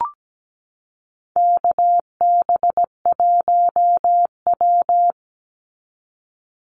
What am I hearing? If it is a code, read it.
Morse code, heard as a keyed tone, about 17 wpm, 700 Hz. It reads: KB1W
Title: Call Signs Repeated 3x - 17wpm